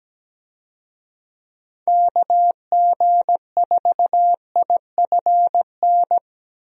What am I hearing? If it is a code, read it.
Morse code, heard as a keyed tone, about 17 wpm, 700 Hz. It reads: KG4IFN
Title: USA Call Signs - 6 Characters - 17wpm